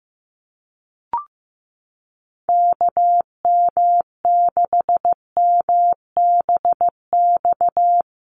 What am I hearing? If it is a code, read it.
Morse code, heard as a keyed tone, about 15 wpm, 700 Hz. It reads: KM6MBX
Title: USA Call Signs Repeated 3x 15wpm